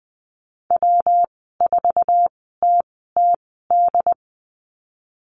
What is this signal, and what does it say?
Morse code, heard as a keyed tone, about 20 wpm, spaced as for 14 wpm, 700 Hz. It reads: W4TTD
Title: 161 Lesson 26g - Letter g - Call Signs 20wpm